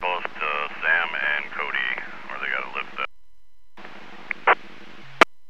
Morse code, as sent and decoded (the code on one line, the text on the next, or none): none